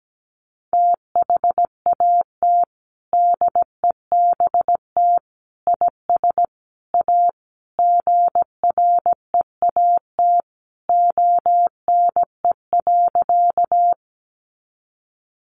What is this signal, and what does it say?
Morse code, heard as a keyed tone, about 17 wpm, 700 Hz. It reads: THAT DEBT IS A GREAT ONE.
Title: Sentences from Top 1300 Words 17wpm